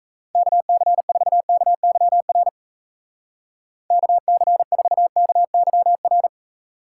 Morse code, heard as a keyed tone, about 34 wpm, 700 Hz. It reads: KC4KYR KC4KYR
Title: Speed Racing - USA 2x3 Call Signs 28wpm